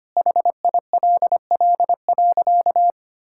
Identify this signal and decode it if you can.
Morse code, heard as a keyed tone, about 25 wpm, 700 Hz. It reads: HILL.
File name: practice / 25wpm / Sentences with 4-Word Phrases - 25wpm